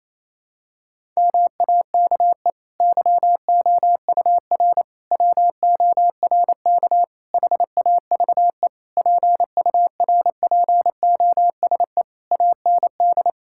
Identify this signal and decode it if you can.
Morse code, heard as a keyed tone, about 28 wpm, 700 Hz. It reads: MAKE YOUR WORK HAVE PURPOSE AND
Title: Sentences from Top 1000 Words - Review Part 6 - 28wpm